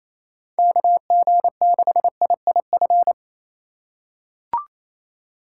Morse code, heard as a keyed tone, about 28 wpm, 700 Hz. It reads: KG6IIF
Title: USA Call Signs - 2x Character Spacing - 28wpm